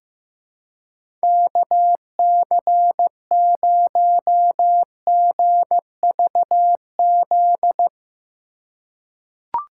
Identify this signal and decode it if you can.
Morse code, heard as a keyed tone, about 15 wpm, 700 Hz. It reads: KC0GVZ
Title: USA Call Signs - 2x Character Spacing - 15wpm